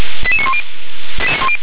Five-tone selective call (ZVEI), heard as quick runs of five tones, tones of about 70 ms. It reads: 88910